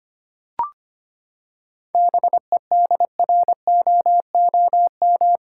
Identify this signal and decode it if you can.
Morse code, heard as a keyed tone, about 25 wpm, 700 Hz. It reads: BEDROOM
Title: Seven Letter Words - 25wpm